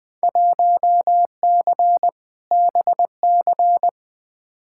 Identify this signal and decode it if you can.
Morse code, heard as a keyed tone, about 20 wpm, 700 Hz. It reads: 1C BC
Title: Field Day Exchanges Encore - 20wpm